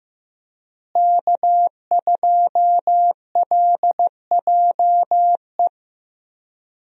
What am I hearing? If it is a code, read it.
Morse code, heard as a keyed tone, about 15 wpm, 700 Hz. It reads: K2LJE
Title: USA Call Signs - 5 Characters - Repeated 3x - 15wpm